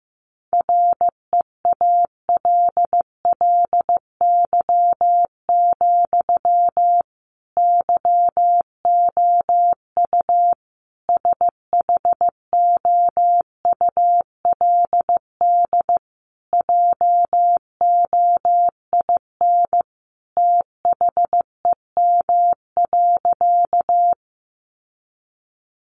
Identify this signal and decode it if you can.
Morse code, heard as a keyed tone, about 15 wpm, 700 Hz. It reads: REALLY, YOU SHOULD JOIN THEM.
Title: Sentences from Top 1400 Words 15wpm